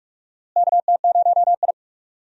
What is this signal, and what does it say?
Morse code, heard as a keyed tone, about 42 wpm, 700 Hz. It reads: KT0I
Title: USA Call Signs - 4 Characters - Repeated 2x - 45wpm